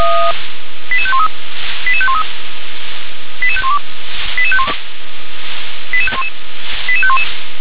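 Five-tone selective call (ZVEI), heard as quick runs of five tones, tones of about 70 ms. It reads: 88513 88513 88512 88511 88511
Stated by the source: ZVEI five-tone call received by SDR.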